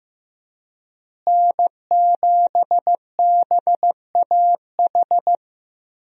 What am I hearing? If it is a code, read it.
Morse code, heard as a keyed tone, about 15 wpm, 700 Hz. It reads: N7BAH
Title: USA Call Signs - 5 Characters - Repeated 2x - 15wpm